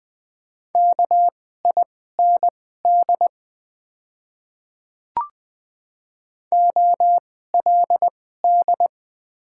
Morse code, heard as a keyed tone, about 20 wpm, spaced as for 14 wpm, 700 Hz. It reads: KIND OLD
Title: 2x Character Spacing - Top 200 Words - 20wpm